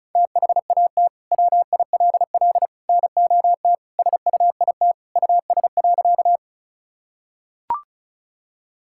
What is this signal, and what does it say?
Morse code, heard as a keyed tone, about 34 wpm, 700 Hz. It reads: THAT WILL NOT SUIT US.
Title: Sentences from Top 1600 Words 35wpm